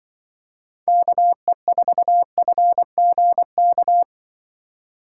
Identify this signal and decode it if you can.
Morse code, heard as a keyed tone, about 25 wpm, 700 Hz. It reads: KE4FGK